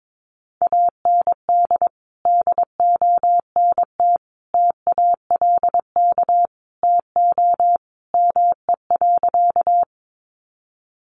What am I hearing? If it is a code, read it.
Morse code, heard as a keyed tone, about 22 wpm, 700 Hz. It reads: AND DONT TALK TO ME.
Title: Sentences from Top 400 Words 22wpm